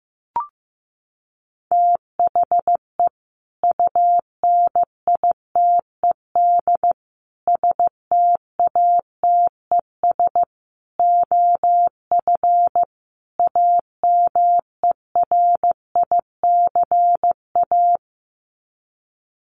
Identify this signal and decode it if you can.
Morse code, heard as a keyed tone, about 15 wpm, 700 Hz. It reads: THE UNITED STATES OF AMERICA
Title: Sets of 5 Words 15wpm